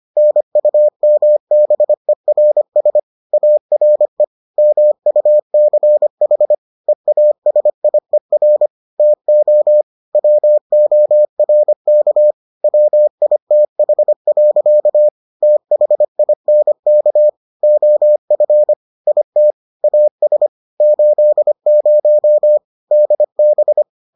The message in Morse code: NUMBERS ARE MUCH EASIER TO WORK WITH. THINK OF IT AS 80 DB

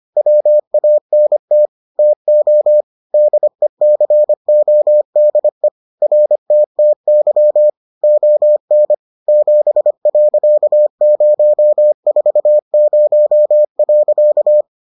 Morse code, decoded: WANT TO DECODE RTTY ON 7.040.

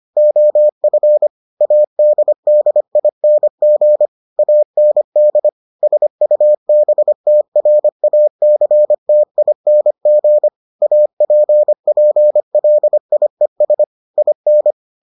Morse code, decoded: OF ADDING AND SUBTRACTING APPLIES IN

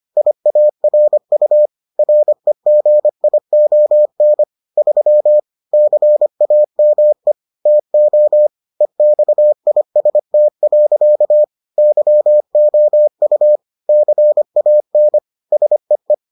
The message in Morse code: IARU REGION 3 CAME TO EXIST. YOU CAN SEE

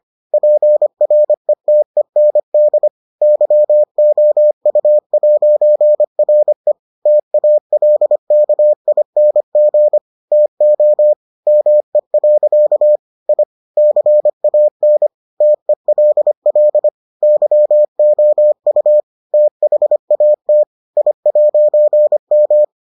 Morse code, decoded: PRETEND YOU'RE TALKING TO ME. I CAN TELL YOU THAT I'M

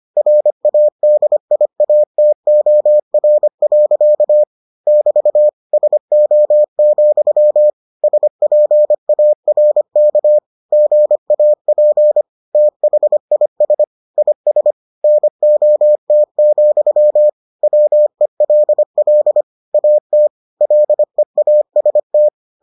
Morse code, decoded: RADIATOR. <BT> SO, SPARK GAP THIS IS NOT, WELL AT LEAST

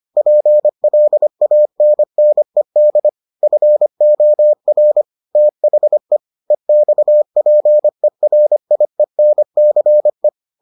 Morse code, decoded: PLANNED FOR THE EXPERIENCE

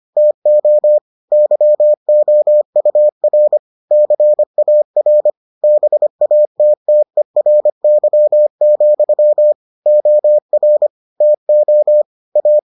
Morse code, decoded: TO YOUR CAR BATTERY, OR TO A

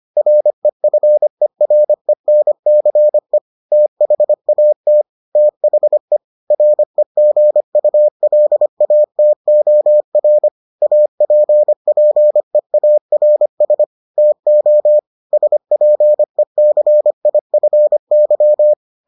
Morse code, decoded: REFERENCE THAT THE REGULATOR APPEARS TO SPECIFY

